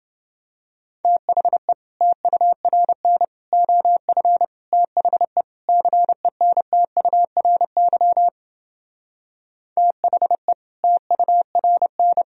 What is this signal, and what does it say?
Morse code, heard as a keyed tone, about 28 wpm, 700 Hz. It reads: THE TURN OF THE CENTURY THE TURN